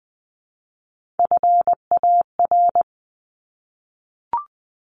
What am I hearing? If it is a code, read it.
Morse code, heard as a keyed tone, about 20 wpm, 700 Hz. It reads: FAR
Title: Top 300 Words 20wpm